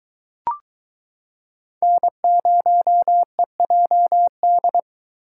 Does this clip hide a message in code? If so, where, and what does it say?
Yes: Morse code from 1.8 to 4.8 s, heard as a keyed tone, about 22 wpm, 700 Hz. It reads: N0EJD